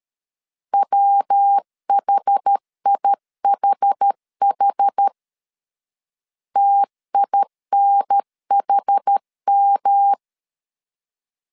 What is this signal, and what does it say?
Morse code, heard as a keyed tone, about 13 wpm, 800 Hz. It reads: WHIHH TINHM
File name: CW / DET / 1 3 1500C S A 12wpm 2905